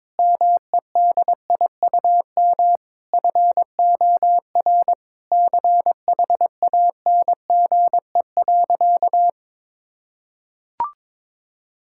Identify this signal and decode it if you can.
Morse code, heard as a keyed tone, about 22 wpm, 700 Hz. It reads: MEDIUM FOR CHANGE.
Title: Sentences from Top 1000 Words - Review Part 6 - 22wpm